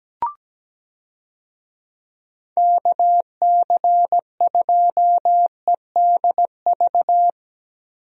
Morse code, heard as a keyed tone, about 17 wpm, 700 Hz. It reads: KC2EDV
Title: USA Call Signs - 6 Characters - 17wpm